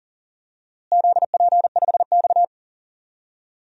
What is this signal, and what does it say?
Morse code, heard as a keyed tone, about 40 wpm, 700 Hz. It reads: ZP5X
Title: DXCC-100 Call Signs - 4 Characters - Repeated 2x - 40wpm